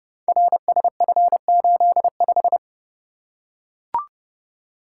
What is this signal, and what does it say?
Morse code, heard as a keyed tone, about 30 wpm, 700 Hz. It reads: RSF85
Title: Code Groups - 5 Letter:Numbers - 30wpm